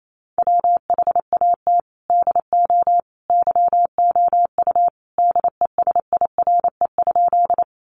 Morse code, heard as a keyed tone, about 28 wpm, 700 Hz. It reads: WHAT DO YOU DESIRE?
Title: Sentences from Top 1100 Words 28wpm